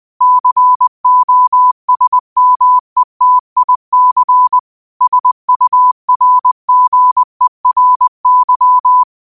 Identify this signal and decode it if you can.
Morse code, heard as a keyed tone, about 20 wpm, 1000 Hz. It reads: COSMETIC SURGERY